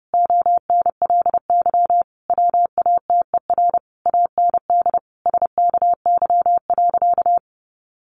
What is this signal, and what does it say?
Morse code, heard as a keyed tone, about 30 wpm, 700 Hz. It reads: ONLY WATER AND SKY.
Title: Sentences from Top 1200 Words 30wpm